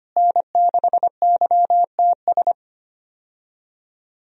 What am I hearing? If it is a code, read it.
Morse code, heard as a keyed tone, about 25 wpm, 700 Hz. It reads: N6YTS
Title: USA Call Signs - 5 Characters - Repeated 3x - 25wpm